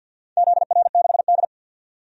Morse code, heard as a keyed tone, about 52 wpm, 700 Hz. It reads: ZR6D
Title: DXCC-100 Call Signs - 4 Characters - 50wpm